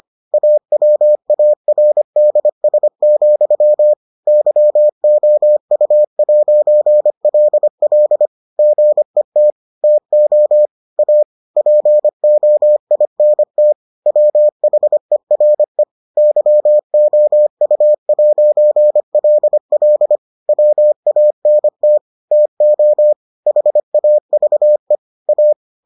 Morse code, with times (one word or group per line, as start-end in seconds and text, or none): 0.3-3.9 AWARDS,
4.3-8.3 YOU'LL
8.6-9.5 GET
9.8-10.7 TO
11.0-11.2 A
11.6-13.7 POINT
14.1-15.8 WHERE
16.2-20.2 YOU'LL
20.5-22.0 WANT
22.3-23.1 TO
23.5-25.0 HAVE
25.3-25.5 A